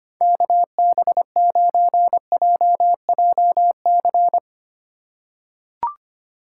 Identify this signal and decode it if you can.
Morse code, heard as a keyed tone, about 25 wpm, 700 Hz. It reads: KB9JJC
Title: USA Call Signs - 6 Characters - 4x Character Spacing - 25wpm